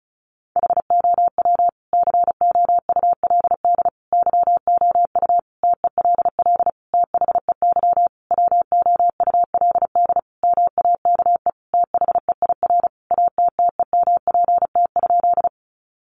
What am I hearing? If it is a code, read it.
Morse code, heard as a keyed tone, about 34 wpm, 700 Hz. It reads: HOW COULD YOU TELL THEY WOULD MAKE THEIR ATTEMPT?
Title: Sentences from Top 1000 Words 35wpm